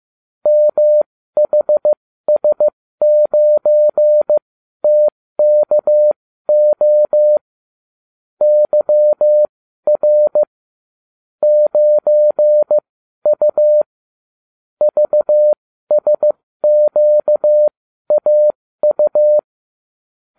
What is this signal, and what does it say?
Morse code, heard as a keyed tone, about 15 wpm, spaced as for 12 wpm, 600 Hz. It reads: MHS9TKO YR 9U VSQAU